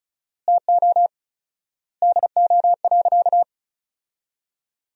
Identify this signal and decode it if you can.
Morse code, heard as a keyed tone, about 36 wpm, 700 Hz. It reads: TO DO.
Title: 4x Word-Spacing - Sentences from Top 200 Words - 35wpm